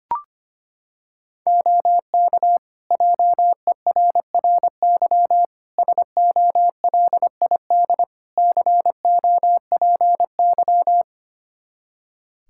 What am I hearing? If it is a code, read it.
Morse code, heard as a keyed tone, about 25 wpm, 700 Hz. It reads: OK JERRY SOLID COPY
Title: Practice QSO - Exchange 3 - More Info - 25wpm